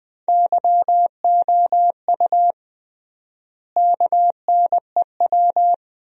Morse code, heard as a keyed tone, about 20 wpm, 700 Hz. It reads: YOU KNEW
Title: 3x Word-Spacing - Sentences from Top 500 Words - 20wpm